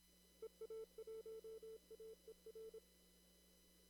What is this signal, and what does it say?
Morse code, heard as a keyed tone, about 24 wpm, 440 Hz. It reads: EA1AER
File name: CW / kt5j wpx cw-op-k5pi